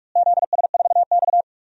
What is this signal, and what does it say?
Morse code, heard as a keyed tone, about 46 wpm, 700 Hz. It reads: ZS4X
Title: DXCC-100 Call Signs - 4 Characters - Repeated 3x - 45wpm